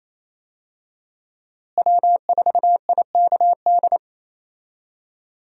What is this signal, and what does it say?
Morse code, heard as a keyed tone, about 28 wpm, 700 Hz. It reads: W4IKD
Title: USA Call Signs - 5 Characters - Repeated 2x - 28wpm